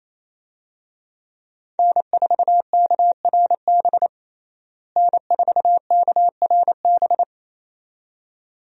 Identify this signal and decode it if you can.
Morse code, heard as a keyed tone, about 28 wpm, 700 Hz. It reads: N4KRB N4KRB
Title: USA Call Signs - 5 Characters - Repeated 2x - 28wpm